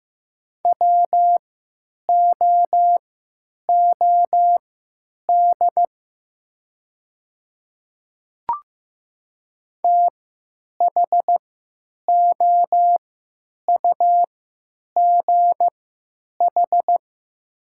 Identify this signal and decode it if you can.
Morse code, heard as a keyed tone, about 15 wpm, spaced as for 9 wpm, 700 Hz. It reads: WOOD THOUGH
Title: 3x Character Spacing - Top 500 Words - 15wpm